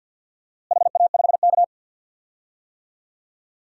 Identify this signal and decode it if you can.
Morse code, heard as a keyed tone, about 51 wpm, 700 Hz. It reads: HA5K
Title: DXCC-100 Call Signs - 4 Characters - Repeated 2x - 50wpm